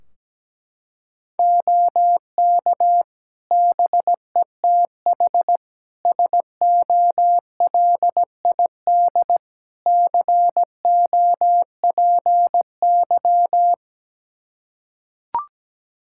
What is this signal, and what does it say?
Morse code, heard as a keyed tone, about 17 wpm, 700 Hz. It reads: OK BETH SOLID COPY
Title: Practice QSO - Exchange 3 - More Info - 17wpm